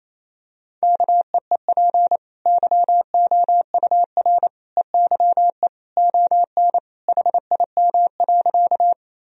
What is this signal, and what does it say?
Morse code, heard as a keyed tone, about 28 wpm, 700 Hz. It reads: KEEP YOUR EYE ON HIM.
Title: Sentences from Top 1000 Words - Review Part 1 - 28wpm